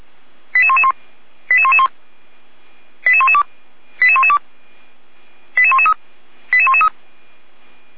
Five-tone selective call (ZVEI), heard as quick runs of five tones, tones of about 70 ms. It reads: 88111 88111 88112 88112 88113 88113
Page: SDR/ZVEI